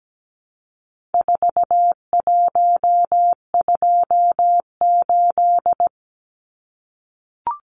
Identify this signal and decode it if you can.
Morse code, heard as a keyed tone, about 17 wpm, 700 Hz. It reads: 4128
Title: Numbers - 4-Digit 17wpm